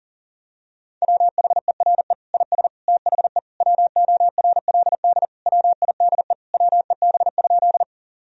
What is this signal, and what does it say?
Morse code, heard as a keyed tone, about 37 wpm, 700 Hz. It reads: WHERE IS THE WORLD WIDE WEB?
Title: Sentences from Top 1000 Words - Review Part 4 - 40wpm